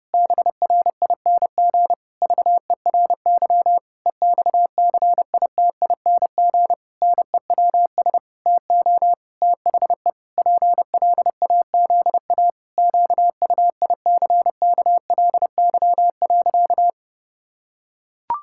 Morse code, heard as a keyed tone, about 30 wpm, 700 Hz. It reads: BRING VERY EXCITING NEWS TO THE PLAZA QUICKLY.
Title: Pangram Challenge v2 - 30wpm